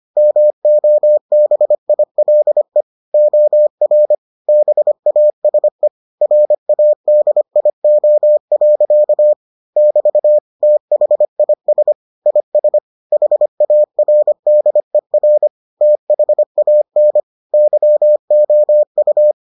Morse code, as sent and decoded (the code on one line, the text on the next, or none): -- --- -... .. .-.. / --- .-. / -... .- ... / .-. .- -.. .. --- .-.-.- / -...- / - .... .. ... / .. ... / .... .- .-. -.. .-. / - .... .- -. / -.-- --- ..-
MOBILE OR BASE RADIO. <BT> THIS IS HARDER THAN YOU